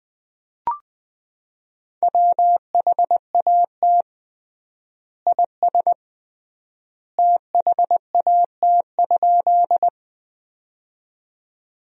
Morse code, heard as a keyed tone, about 21 wpm, 700 Hz. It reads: WHAT IS THAT?
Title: 3x Word-Spacing - Sentences from Top 100 Words - 20wpm